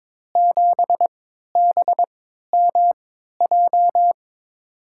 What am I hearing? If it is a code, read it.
Morse code, heard as a keyed tone, about 23 wpm, spaced as for 13 wpm, 700 Hz. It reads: 7BMJ
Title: USA Call Signs - 5 Characters - 3x Character Spacing - 22wpm